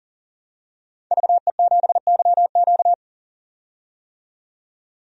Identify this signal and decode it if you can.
Morse code, heard as a keyed tone, about 40 wpm, 700 Hz. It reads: VE7YQ
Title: DXCC-100 Call Signs - 5 Characters - 40wpm